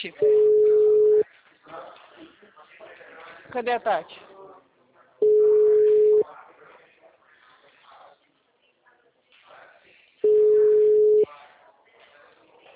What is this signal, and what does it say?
Ringback tone (Europe, 425 Hz), with speech.
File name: out- -2007- - - .37800